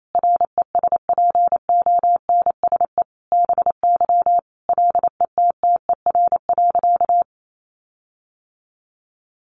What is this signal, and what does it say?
Morse code, heard as a keyed tone, about 28 wpm, 700 Hz. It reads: RESPONSE BY LETTER.
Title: Sentences from Top 1400 Words 28wpm